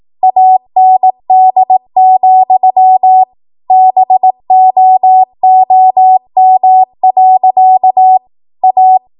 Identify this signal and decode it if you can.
Morse code, heard as a keyed tone, about 18 wpm, 750 Hz. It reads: AND, BOOM. A
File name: Morse / 18WPM